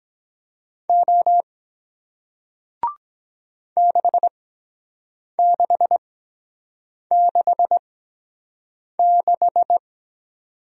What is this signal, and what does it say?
Morse code, heard as a keyed tone, about 23 wpm, spaced as for 7 wpm, 700 Hz. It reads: O 6666